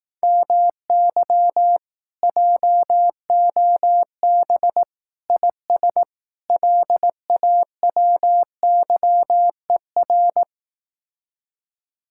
Morse code, heard as a keyed tone, about 18 wpm, 700 Hz. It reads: MY JOB IS LAWYER
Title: Speed-Racing Common QSO Phrases Encore - 15wpm